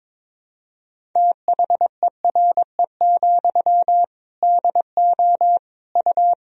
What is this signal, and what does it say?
Morse code, heard as a keyed tone, about 22 wpm, 700 Hz. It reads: THERE, DO U
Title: Sentences from Top 300 Words and Abbreviations 22wpm